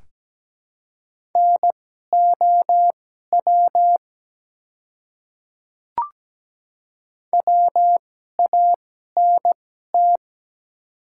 Morse code, heard as a keyed tone, about 17 wpm, spaced as for 12 wpm, 700 Hz. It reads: NOW WANT
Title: 2x Character Spacing - Top 300 Words - 17wpm